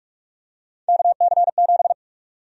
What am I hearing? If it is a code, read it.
Morse code, heard as a keyed tone, about 45 wpm, 700 Hz. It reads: KC7
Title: USA Call Sign Prefixes - Repeated 2x - 45wpm